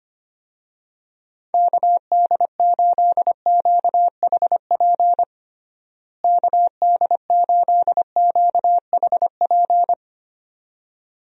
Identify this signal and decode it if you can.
Morse code, heard as a keyed tone, about 25 wpm, 700 Hz. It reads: KD8QHP KD8QHP
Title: USA Call Signs - 6 Characters - Repeated 2x - 25wpm